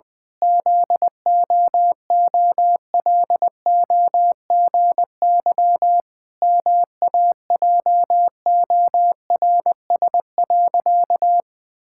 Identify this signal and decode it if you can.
Morse code, heard as a keyed tone, about 20 wpm, 700 Hz. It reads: ZOOLOGY MAJORS.
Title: Pangram Challenge v2 - 20wpm